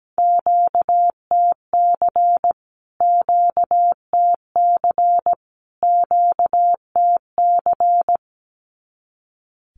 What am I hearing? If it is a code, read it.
Morse code, heard as a keyed tone, about 17 wpm, 700 Hz. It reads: QTC QTC QTC
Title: Q-Codes Repeated 3x 17wpm